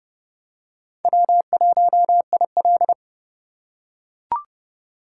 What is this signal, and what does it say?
Morse code, heard as a keyed tone, about 29 wpm, 700 Hz. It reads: W1IL